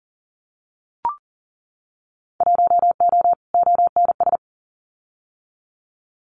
Morse code, heard as a keyed tone, about 40 wpm, 700 Hz. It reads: 1O ONS
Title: Winter Field Day Exchanges 40wpm